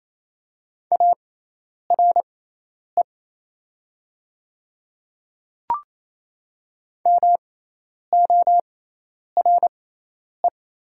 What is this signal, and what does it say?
Morse code, heard as a keyed tone, about 27 wpm, spaced as for 10 wpm, 700 Hz. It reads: ARE MORE
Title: 6x Character Spacing - Top 200 Words - 28wpm